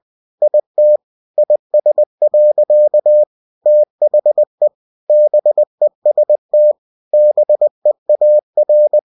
Morse code, heard as a keyed tone, about 20 wpm, 600 Hz. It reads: IT IS. THE BEST BEAR
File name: cw-books / winnie-the-pooh / winnie-20wpm-page-0108